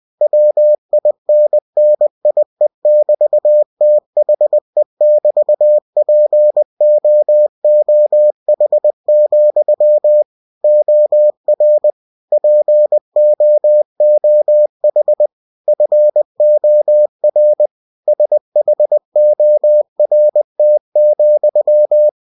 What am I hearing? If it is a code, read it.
Morse code, heard as a keyed tone, about 20 wpm, 600 Hz. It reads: WINNIE<BT>THE<BT>POOH, OR POOH FOR SHORT,